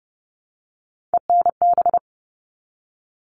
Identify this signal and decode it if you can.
Morse code, heard as a keyed tone, about 30 wpm, 700 Hz. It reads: ENB
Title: All Cut Numbers - 3-Digit 30wpm